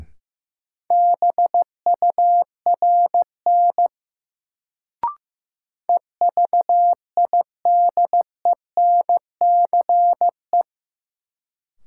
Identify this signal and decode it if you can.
Morse code, heard as a keyed tone, about 15 wpm, 700 Hz. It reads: BURN EVIDENCE
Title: Sentences from Top 1300 Words - New Word Drill Alt - 15wpm